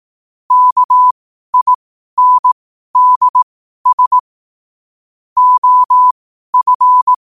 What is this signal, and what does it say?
Morse code, heard as a keyed tone, about 18 wpm, spaced as for 12 wpm, 1000 Hz. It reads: KINDS OF
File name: qotdfiles / 12wpm